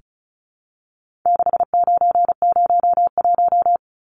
Morse code, heard as a keyed tone, about 34 wpm, 700 Hz. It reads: 6901
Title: Numbers - 4-Digit 35wpm